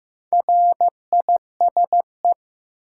Morse code, heard as a keyed tone, about 16 wpm, 700 Hz. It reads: RISE
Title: Sentences from Top 900 Words - New Word Drill - 15wpm